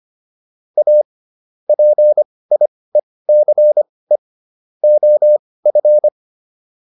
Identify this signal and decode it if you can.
Morse code, heard as a keyed tone, about 25 wpm, spaced as for 18 wpm, 600 Hz. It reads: A PIECE OF